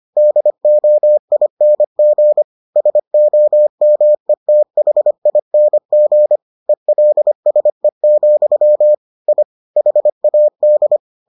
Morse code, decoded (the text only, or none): DOING SOMETHING ELSE, I HAD